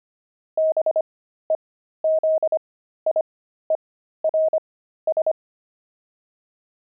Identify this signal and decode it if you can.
Morse code, heard as a keyed tone, about 25 wpm, spaced as for 13 wpm, 630 Hz. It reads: BEZIERS
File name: pangram-fs-13-cs-25-zaph